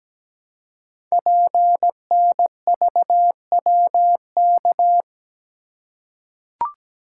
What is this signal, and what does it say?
Morse code, heard as a keyed tone, about 17 wpm, 700 Hz. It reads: PNVWK
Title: Code Groups - 5 Letters - 17wpm